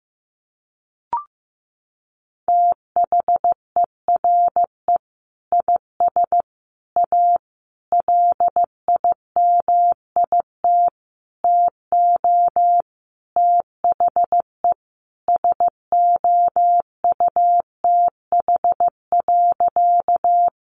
Morse code, heard as a keyed tone, about 14 wpm, 700 Hz. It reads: THERE IS A LIMIT TO THE SOUTH.
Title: Sentences from Top 1500 Words 15wpm